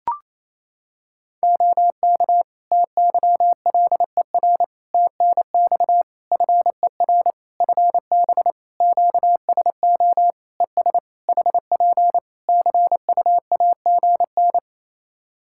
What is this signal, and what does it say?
Morse code, heard as a keyed tone, about 28 wpm, 700 Hz. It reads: OK TYLER TNX FER FB QSO ES HP CUAGN
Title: Practice QSO - Exchange 4 - Ending - 28wpm